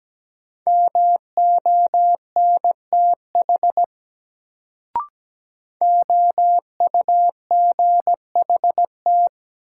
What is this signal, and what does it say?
Morse code, heard as a keyed tone, about 17 wpm, 700 Hz. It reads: MONTH OUGHT
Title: Sets of 5 Words Encore - New Word Drill - 17wpm